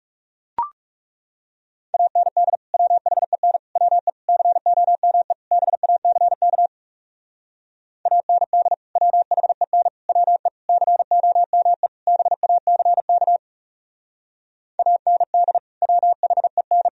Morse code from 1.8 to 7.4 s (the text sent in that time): AND WHEN WE COME BACK